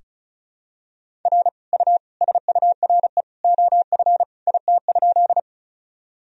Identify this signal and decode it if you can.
Morse code, heard as a keyed tone, about 36 wpm, 700 Hz. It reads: R U SURE OF IT?
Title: Sentences from Top 300 Words and Abbreviations 35wpm